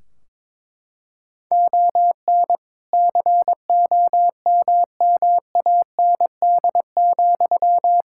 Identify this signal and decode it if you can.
Morse code, heard as a keyed tone, about 21 wpm, 700 Hz. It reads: ON COMMAND,
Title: Sentences from Top 1600 Words 22wpm